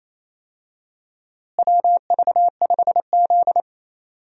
Morse code, heard as a keyed tone, about 28 wpm, 700 Hz. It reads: WV5Z
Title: USA Call Signs - 4 Characters - Repeated 3x - 28wpm